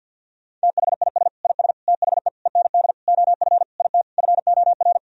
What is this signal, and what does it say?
Morse code, heard as a keyed tone, about 51 wpm, 700 Hz. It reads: THIS IS THE END OF IT FOR
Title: Sentences from Top 1000 Words - Review Part 3 - 50wpm